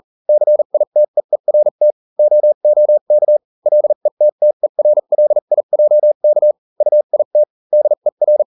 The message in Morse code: CITEERT OOK LETTERLIJK UIT DER